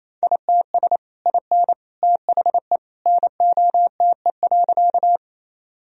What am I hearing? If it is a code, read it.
Morse code, heard as a keyed tone, about 27 wpm, 700 Hz. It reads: ITS IN THE NOTE.
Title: Sentences from Top 1000 Words - Review Part 2 - 28wpm